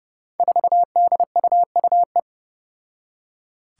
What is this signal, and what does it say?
Morse code, heard as a keyed tone, about 32 wpm, 700 Hz. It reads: 4DUUE
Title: Five Challenging Letter-Number Combinations 30wpm